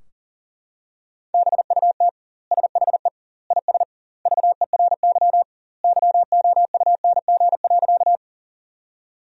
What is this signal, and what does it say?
Morse code, heard as a keyed tone, about 40 wpm, 700 Hz. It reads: BUT SHE IS VERY YOUNG.
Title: 2x Word-Spacing - Sentences from Top 400 Words - 40wpm